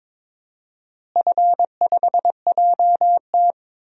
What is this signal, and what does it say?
Morse code, heard as a keyed tone, about 22 wpm, 700 Hz. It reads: F5JT
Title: DXCC-100 Call Signs - 4 Characters - Repeated 3x - 22wpm